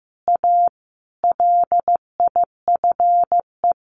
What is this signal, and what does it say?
Morse code, heard as a keyed tone, about 15 wpm, 700 Hz. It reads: A LIFE
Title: Sentences from Top 1400 Words 15wpm